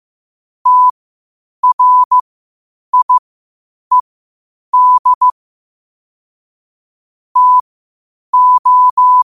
Morse code, heard as a keyed tone, about 15 wpm, spaced as for 8 wpm, 1000 Hz. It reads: TRIED TO
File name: qotdfiles / 8wpm